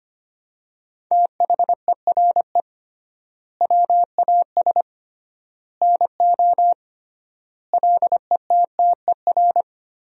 Morse code, heard as a keyed tone, about 25 wpm, 700 Hz. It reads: THERE WAS NO LETTER
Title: 3x Word-Spacing - Sentences from Top 300 Words - 25wpm